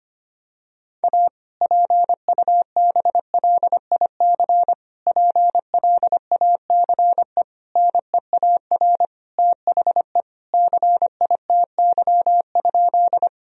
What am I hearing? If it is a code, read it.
Morse code, heard as a keyed tone, about 25 wpm, 700 Hz. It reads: A PUBLIC PLACE NEAR THE CITY?
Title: Sentences from Top 1000 Words - Review Part 3 - 25wpm